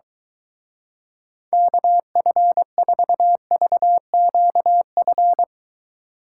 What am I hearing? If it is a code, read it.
Morse code, heard as a keyed tone, about 23 wpm, 700 Hz. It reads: KF4VQF